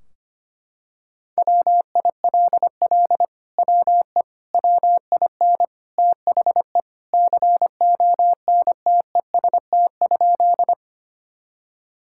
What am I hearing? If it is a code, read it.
Morse code, heard as a keyed tone, about 25 wpm, 700 Hz. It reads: WILL WE WIN THE CONTEST?
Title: Sentences from Top 1600 Words 25wpm